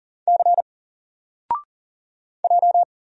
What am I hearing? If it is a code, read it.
Morse code, heard as a keyed tone, about 40 wpm, 700 Hz. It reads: C J
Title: Single Letter-Number Alt 40wpm